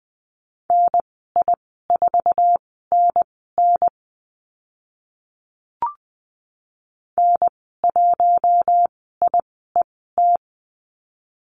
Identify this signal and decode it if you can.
Morse code, heard as a keyed tone, about 20 wpm, spaced as for 14 wpm, 700 Hz. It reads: NI4NN N1IET
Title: 076 Lesson 13g - Letter L - Call Signs 20wpm